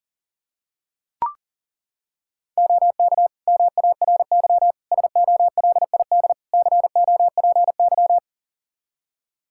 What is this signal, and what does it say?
Morse code, heard as a keyed tone, about 37 wpm, 700 Hz. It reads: OK MARY SOLID COPY
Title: Practice QSO - Exchange 3 - More Info - 40wpm